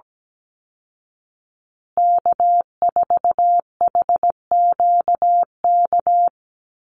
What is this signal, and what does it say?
Morse code, heard as a keyed tone, about 17 wpm, 700 Hz. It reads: K4HQK
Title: Call Signs Repeated 2x - 17wpm